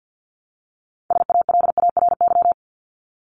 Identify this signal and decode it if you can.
Morse code, heard as a keyed tone, about 50 wpm, 700 Hz. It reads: SALARY
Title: Top 1751-2000 Words 50wpm